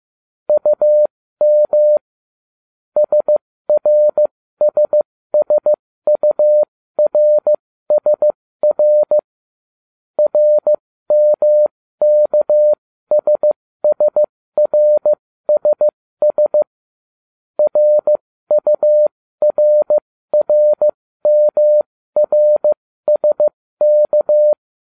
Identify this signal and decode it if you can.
Morse code, heard as a keyed tone, about 15 wpm, spaced as for 13 wpm, 600 Hz. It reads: UM SRSSURSR RMKSSRSS RURRMRSK